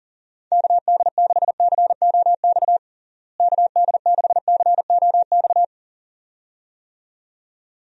Morse code, heard as a keyed tone, about 40 wpm, 700 Hz. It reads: KD6COX KD6COX
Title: USA Call Signs - 6 Characters - Repeated 2x - 40wpm